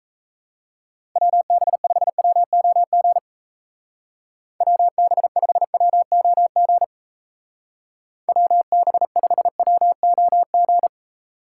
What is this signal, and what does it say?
Morse code, heard as a keyed tone, about 34 wpm, 700 Hz. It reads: WB5WOG WB5WOG WB5WOG
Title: Speed Racing - USA 2x3 Call Signs 28wpm